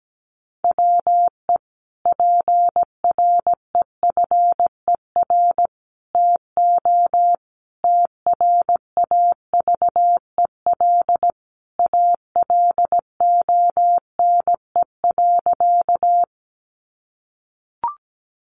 Morse code, heard as a keyed tone, about 17 wpm, 700 Hz. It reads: WE PREFER TO TRAVEL ALONE.
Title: Sentences from Top 1100 Words 17wpm